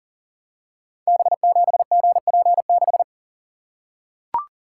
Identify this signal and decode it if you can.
Morse code, heard as a keyed tone, about 38 wpm, 700 Hz. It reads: B7GP6
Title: Code Groups - 5 Letter:Numbers - 40wpm